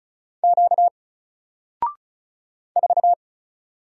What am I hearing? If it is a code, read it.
Morse code, heard as a keyed tone, about 36 wpm, 700 Hz. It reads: Q 4
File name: Single-Letter-Number-35wpm